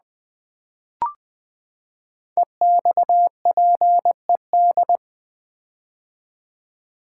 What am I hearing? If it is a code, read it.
Morse code, heard as a keyed tone, about 20 wpm, 700 Hz. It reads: EXPED
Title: Build A Word - 5 Letters At A Time - 10-Letter Words - 20wpm